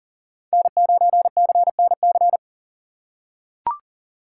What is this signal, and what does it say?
Morse code, heard as a keyed tone, about 40 wpm, 700 Hz. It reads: N9CNC
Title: USA Call Signs - 2x Character Spacing - 40wpm